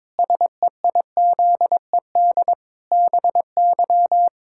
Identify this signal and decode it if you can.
Morse code, heard as a keyed tone, about 22 wpm, 700 Hz. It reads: SEIZED BY
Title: Pangram Challenge v2 - 22wpm